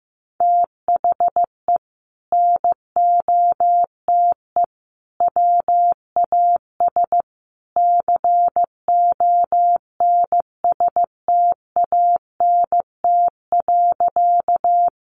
Morse code, heard as a keyed tone, about 15 wpm, 700 Hz. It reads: THE NOTE WAS CONSTANT.
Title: Sentences from Top 1500 Words 15wpm